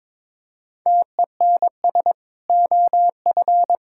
Morse code, heard as a keyed tone, about 22 wpm, 700 Hz. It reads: TENS OF
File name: practice / 22wpm / Sentences with 4-Word Phrases - 22wpm